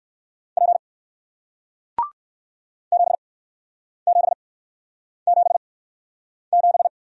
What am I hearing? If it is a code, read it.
Morse code, heard as a keyed tone, about 61 wpm, 700 Hz. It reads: P 7 7 7 7